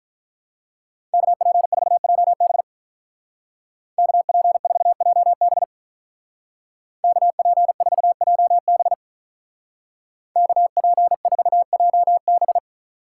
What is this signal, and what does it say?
Morse code, heard as a keyed tone, about 43 wpm, 700 Hz. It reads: KP4JB KP4JB KP4JB KP4JB
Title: Speed Racing - USA 2x2 Call Signs 35wpm